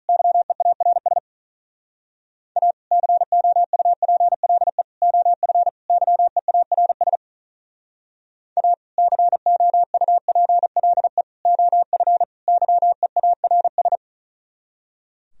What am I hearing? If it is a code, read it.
Morse code, heard as a keyed tone, about 45 wpm, 700 Hz. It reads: YEARS A COUPLE OF YEARS A COUPLE OF YEARS